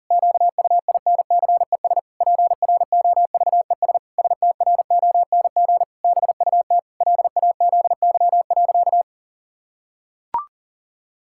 Morse code, heard as a keyed tone, about 40 wpm, 700 Hz. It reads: QUINCES PROVES STRONG BUT LAZY.
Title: Pangram Challenge v2 - 40wpm